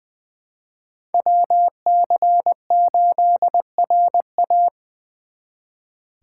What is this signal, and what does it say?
Morse code, heard as a keyed tone, about 24 wpm, 700 Hz. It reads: WC8RA
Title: Speed Racing - USA 2x2 Call Signs 20wpm